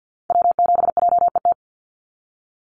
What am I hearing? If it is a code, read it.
Morse code, heard as a keyed tone, about 50 wpm, 700 Hz. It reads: W7JET
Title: NA SOTA Activator Callsigns 50wpm